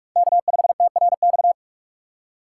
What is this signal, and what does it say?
Morse code, heard as a keyed tone, about 43 wpm, 700 Hz. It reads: K5TRX
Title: USA Call Signs - 5 Characters - 3x Character Spacing - 45wpm